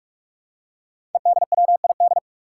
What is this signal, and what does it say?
Morse code, heard as a keyed tone, about 44 wpm, 700 Hz. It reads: EDWID
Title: Five Challenging Letter-Number Combinations 45wpm